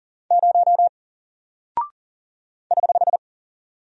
Morse code, heard as a keyed tone, about 40 wpm, 700 Hz. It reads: 0 <HH>
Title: Single Character (Letter:Number:Punctuation:Prosign) 40wpm